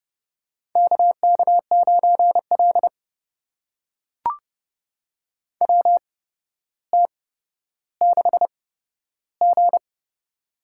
Morse code, heard as a keyed tone, about 30 wpm, spaced as for 13 wpm, 700 Hz. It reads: KK9L WT6G
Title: USA Call Signs - 4 Characters - 8x Character Spacing - 30wpm